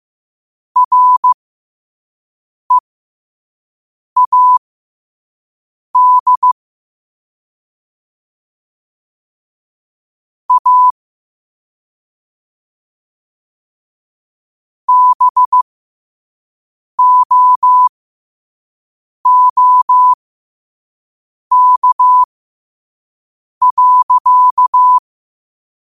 Morse code, heard as a keyed tone, about 15 wpm, spaced as for 5 wpm, 1000 Hz. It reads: READ A BOOK.
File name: qotdfiles / 5wpm